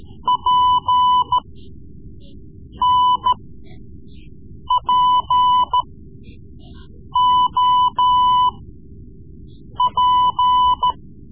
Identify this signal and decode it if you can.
Morse code, heard as a keyed tone, about 14 wpm, spaced as for 5 wpm, 1000 Hz. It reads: PNPOP
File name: Morse Lessons / Morse Lessons / Tape-3B